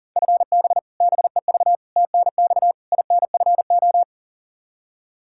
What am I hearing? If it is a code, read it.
Morse code, heard as a keyed tone, about 40 wpm, 700 Hz. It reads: FB BEV TNX INFO
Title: Practice QSO - Exchange 3 - More Info - 40wpm